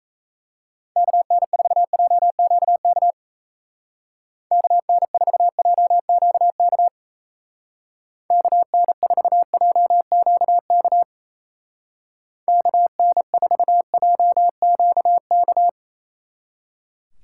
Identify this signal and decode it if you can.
Morse code, heard as a keyed tone, about 34 wpm, 700 Hz. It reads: KN4JQK KN4JQK KN4JQK KN4JQK
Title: Speed Racing - USA 2x3 Call Signs 28wpm